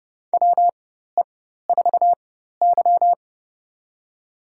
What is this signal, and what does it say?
Morse code, heard as a keyed tone, about 30 wpm, spaced as for 14 wpm, 700 Hz. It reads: WE4Y
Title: USA Call Signs - 4 Characters - 4x Character Spacing - 30wpm